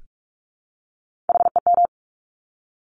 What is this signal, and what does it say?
Morse code, heard as a keyed tone, about 46 wpm, 700 Hz. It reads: HEM
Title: Top Three Letter Combinations in Words 45wpm